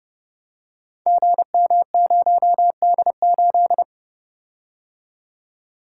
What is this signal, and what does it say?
Morse code, heard as a keyed tone, about 30 wpm, 700 Hz. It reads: GM0D8
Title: Code Groups - 5 Letter:Numbers - 30wpm